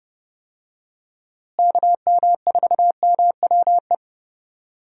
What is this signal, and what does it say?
Morse code, heard as a keyed tone, about 30 wpm, 700 Hz. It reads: KM4MWE
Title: USA Call Signs - 6 Characters - Repeated 3x - 30wpm